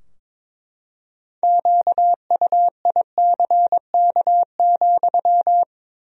Morse code, heard as a keyed tone, about 22 wpm, 700 Hz. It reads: QUICK,
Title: 2x Word-Spacing - Sentences from Top 500 Words - 22wpm